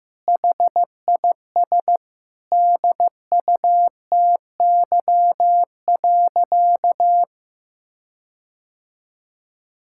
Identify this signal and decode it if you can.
Morse code, heard as a keyed tone, about 15 wpm, 700 Hz. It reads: HIS DUTY.
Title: Sentences from Top 1600 Words 15wpm